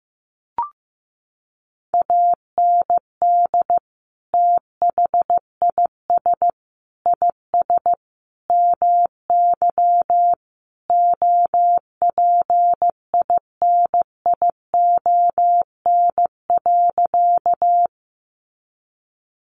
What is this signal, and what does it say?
Morse code, heard as a keyed tone, about 15 wpm, 700 Hz. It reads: AND THIS IS MY OPINION.
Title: Sentences from Top 1200 Words 15wpm